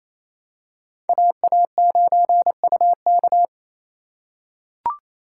Morse code, heard as a keyed tone, about 28 wpm, 700 Hz. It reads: AA9UK
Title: USA Call Signs - 5 Characters - 2x Character Spacing - 28wpm